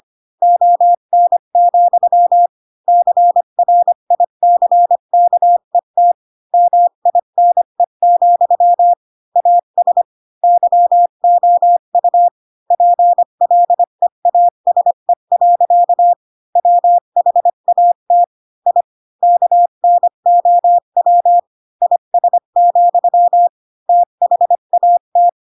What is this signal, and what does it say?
Morse code, heard as a keyed tone, about 25 wpm, 700 Hz. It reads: ON, CRICKET MINE, AS YOU PLEASE. WHAT I KNOW IS, THAT
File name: Pinocchio / Pinocchio ch4 20 25 ik1hge